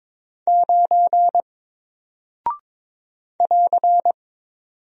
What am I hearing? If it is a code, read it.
Morse code, heard as a keyed tone, about 22 wpm, 700 Hz. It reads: 9 <AR>
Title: Single Character (Letter:Number:Punctuation:Prosign) Alt 22wpm